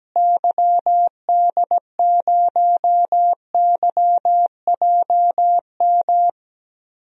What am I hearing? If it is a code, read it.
Morse code, heard as a keyed tone, about 17 wpm, 700 Hz. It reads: YD0YJM
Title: DXCC-100 Call Signs - 6 Characters - Repeated 3x - 17wpm